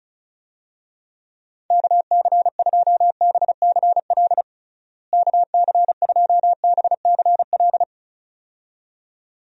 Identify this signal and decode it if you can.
Morse code, heard as a keyed tone, about 36 wpm, 700 Hz. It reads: KC2BCL KC2BCL
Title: USA Call Signs - 6 Characters - Repeated 2x - 35wpm